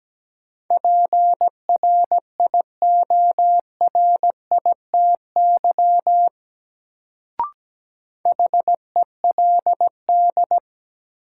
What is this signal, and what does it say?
Morse code, heard as a keyed tone, about 17 wpm, 700 Hz. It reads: PRIORITY HELD
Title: Sentences from Top 1600 Words - New Word Drill - 17wpm